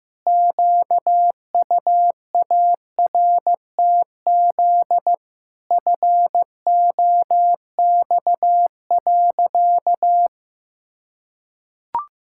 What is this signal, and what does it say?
Morse code, heard as a keyed tone, about 15 wpm, 700 Hz. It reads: QUARTZ FOX.
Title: Pangram Challenge v2 - 15wpm